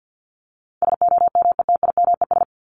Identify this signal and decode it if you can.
Morse code, heard as a keyed tone, about 50 wpm, 700 Hz. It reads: SOMETIMES
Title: ES:S Suffix - Top 100 Words with Base Words 50wpm